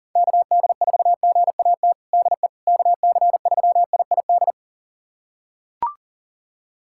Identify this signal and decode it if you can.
Morse code, heard as a keyed tone, about 40 wpm, 700 Hz. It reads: KD4GAT DE KC3IID
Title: USA Callsign DE Callsign - 6 Characters - 40wpm